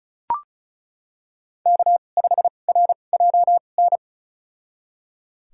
Morse code, heard as a keyed tone, about 36 wpm, spaced as for 25 wpm, 700 Hz. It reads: K5RJN
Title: USA Call Signs - 5 Characters - 2x Character Spacing - 35wpm